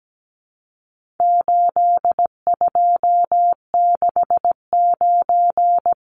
Morse code, heard as a keyed tone, about 17 wpm, 700 Hz. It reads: 8269
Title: Numbers - 4-Digit 17wpm